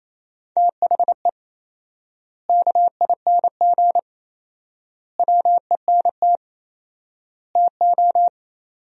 Morse code, heard as a keyed tone, about 28 wpm, 700 Hz. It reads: THE KING WENT TO
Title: 4x Word-Spacing - Sentences from Top 400 Words - 28wpm